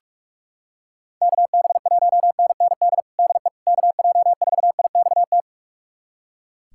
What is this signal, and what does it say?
Morse code, heard as a keyed tone, about 42 wpm, 700 Hz. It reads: KB1NND DE KJ4IXT
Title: USA Callsign DE Callsign - 6 Characters - 45wpm